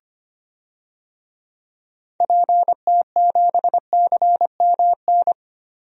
Morse code, heard as a keyed tone, about 25 wpm, 700 Hz. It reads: PT7CMN
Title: DXCC-100 Call Signs - 6 Characters - Repeated 2x - 25wpm